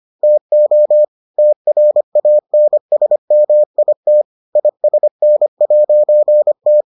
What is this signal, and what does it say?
Morse code, heard as a keyed tone, about 25 wpm, 600 Hz. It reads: TO TRANSMIT ISN'T